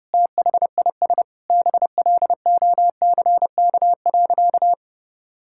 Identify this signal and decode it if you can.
Morse code, heard as a keyed tone, about 30 wpm, 700 Hz. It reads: THIS BLOCK.
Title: Sentences from Top 1600 Words 30wpm